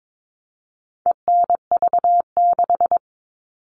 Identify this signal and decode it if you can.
Morse code, heard as a keyed tone, about 22 wpm, 700 Hz. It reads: ENV6
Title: All Cut Numbers - 4-Digit 22wpm